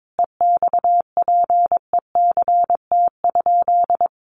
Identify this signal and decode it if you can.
Morse code, heard as a keyed tone, about 22 wpm, 700 Hz. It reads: EXPECT?
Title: Sentences from Top 1300 Words 22wpm